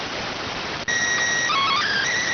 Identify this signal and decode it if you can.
SSTV (slow-scan television), PD 290.